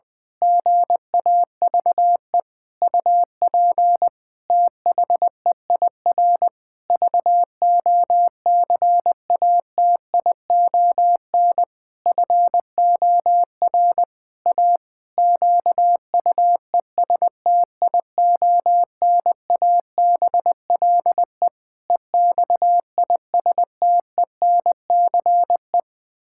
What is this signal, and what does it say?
Morse code, heard as a keyed tone, about 20 wpm, 700 Hz. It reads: GAVE UP THEIR VOCATION FOR A QUESTIONABLE EXISTENCE
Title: Pangram Challenge v2 - 20wpm